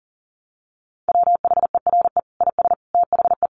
Morse code, heard as a keyed tone, about 40 wpm, 700 Hz. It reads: WHERE IS THE
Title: Sentences from Top 1200 Words 40wpm